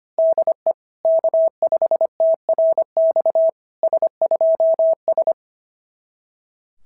Morse code, heard as a keyed tone, about 22 wpm, 650 Hz. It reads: DE K5TRX S2S
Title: SOTA QSO Practice 25wpm